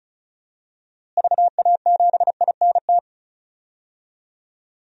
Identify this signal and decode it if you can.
Morse code, heard as a keyed tone, about 36 wpm, 700 Hz. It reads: VA7INT
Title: DXCC-100 Call Signs - 6 Characters - Repeated 3x - 35wpm